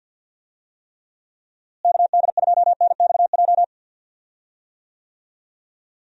Morse code, heard as a keyed tone, about 51 wpm, 700 Hz. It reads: KD2NXJ
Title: USA Call Signs - 6 Characters - Repeated 2x - 50wpm